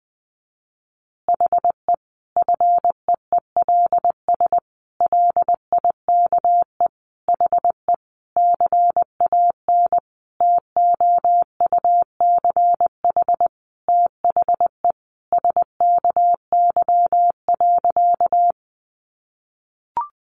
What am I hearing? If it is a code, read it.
Morse code, heard as a keyed tone, about 19 wpm, 700 Hz. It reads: HE FEELS LIKE HE CAN TOUCH THE SKY.
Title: Additional Easy Phrases 20wpm